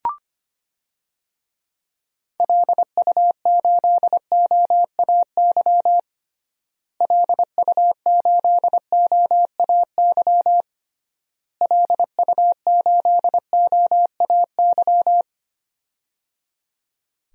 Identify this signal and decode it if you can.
Morse code, heard as a keyed tone, about 25 wpm, 700 Hz. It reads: LU8OAY LU8OAY LU8OAY
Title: DXCC-100 Call Signs - 6 Characters - Repeated 3x - 25wpm